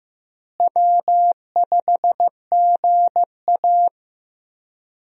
Morse code, heard as a keyed tone, about 15 wpm, 700 Hz. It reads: W5GA
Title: USA Call Signs - 4 Characters - Repeated 2x - 15wpm